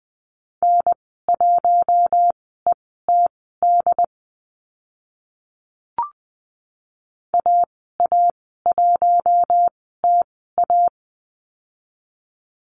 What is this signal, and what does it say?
Morse code, heard as a keyed tone, about 20 wpm, spaced as for 15 wpm, 700 Hz. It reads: N1ETD AA1TA
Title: 069 Lesson 12g - Letter d - Call Signs 20wpm